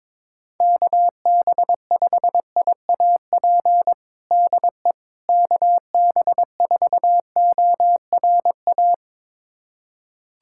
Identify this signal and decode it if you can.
Morse code, heard as a keyed tone, about 22 wpm, 700 Hz. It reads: KB5IAP DE KB4ORA
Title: USA Callsign DE Callsign - 6 Characters - 22wpm